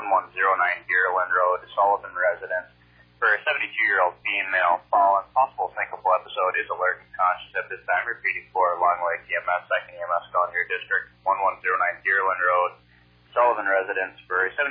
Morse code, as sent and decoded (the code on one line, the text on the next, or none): none